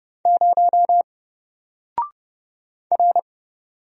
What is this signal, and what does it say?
Morse code, heard as a keyed tone, about 30 wpm, 700 Hz. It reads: 0 R
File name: Single-Letter-Number-30wpm